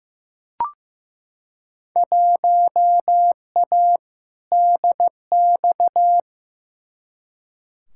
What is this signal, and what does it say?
Morse code, heard as a keyed tone, about 15 wpm, 700 Hz. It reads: 1A DX
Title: Field Day Exchanges Encore - 15wpm